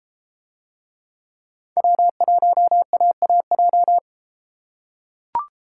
Morse code, heard as a keyed tone, about 32 wpm, 700 Hz. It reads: W1AAJ